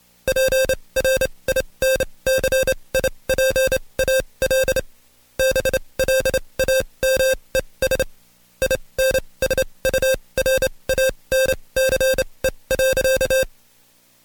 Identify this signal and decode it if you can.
Morse code, heard as a keyed tone, about 27 wpm, spaced as for 22 wpm, 530 Hz. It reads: PRINCIPAL BLAMES INSURANCE.
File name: Code Quick / AudioDownld2 / Seg137-Stepper10 21WPM